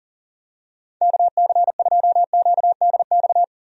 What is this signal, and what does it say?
Morse code, heard as a keyed tone, about 40 wpm, 690 Hz. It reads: KC2QDX
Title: USA Call Signs - 6 Characters - 40wpm